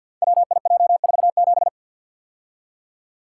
Morse code, heard as a keyed tone, about 50 wpm, 700 Hz. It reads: WIJ47
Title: Code Groups - 5 Letter:Numbers - 50wpm